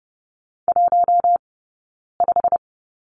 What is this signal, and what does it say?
Morse code, heard as a keyed tone, about 30 wpm, spaced as for 9 wpm, 690 Hz. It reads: 15
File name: Lesson 17d - Letter u - Two Characters 30wpm